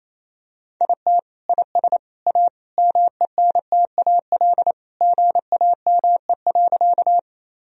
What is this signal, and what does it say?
Morse code, heard as a keyed tone, about 28 wpm, 700 Hz. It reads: IT IS A MENTAL GAME.
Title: Sentences from Top 1000 Words - Review Part 4 - 28wpm